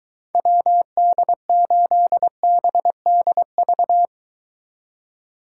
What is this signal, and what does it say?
Morse code, heard as a keyed tone, about 24 wpm, 700 Hz. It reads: WD8BDV